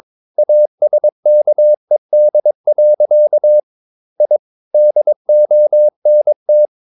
Morse code, heard as a keyed tone, about 22 wpm, 600 Hz. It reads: ASKED. I DONT
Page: Romanen Never af Ken Follet på engelsk.